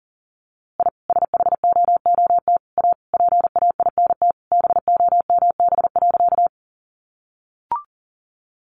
Morse code, heard as a keyed tone, about 40 wpm, 700 Hz. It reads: I SHOOT A PAINT BOMB.
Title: A Sound Of Thunder 40wpm